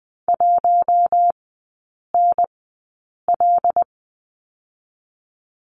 Morse code, heard as a keyed tone, about 20 wpm, spaced as for 8 wpm, 700 Hz. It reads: 1NL
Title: 184 Lesson 30e - Letter b - Three Characters 20wpm